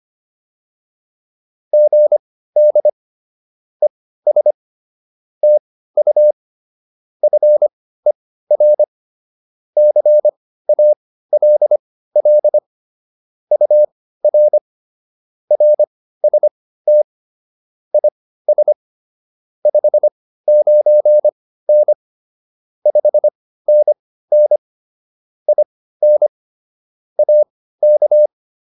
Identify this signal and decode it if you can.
Morse code, heard as a keyed tone, about 25 wpm, spaced as for 15 wpm, 600 Hz. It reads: GD ES TU FER CALL UR RST IS 59N 5NN IN AK
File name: QSO107 15